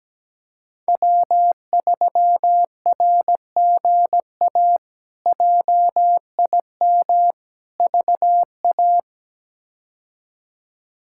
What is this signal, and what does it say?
Morse code, heard as a keyed tone, about 17 wpm, 700 Hz. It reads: W3RGA JIM VA
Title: K1USN SST QSO Practice - 17wpm